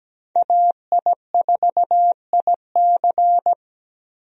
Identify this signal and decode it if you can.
Morse code, heard as a keyed tone, about 17 wpm, 700 Hz. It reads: AI4IC
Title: USA Call Signs - 5 Characters - 17wpm